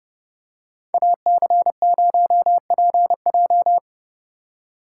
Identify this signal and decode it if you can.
Morse code, heard as a keyed tone, about 31 wpm, 700 Hz. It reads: AC0PJ